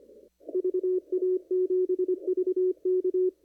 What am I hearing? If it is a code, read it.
Morse code, heard as a keyed tone, about 24 wpm, 370 Hz. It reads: VA7VK
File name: CW / kt5j wpx cw-op-k5pi